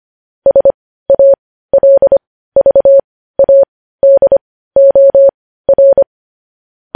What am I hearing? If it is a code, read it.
Morse code, heard as a keyed tone, about 25 wpm, spaced as for 15 wpm, 550 Hz. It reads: SALVADOR
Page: news-headlines-in-morse-code-at-15-wpm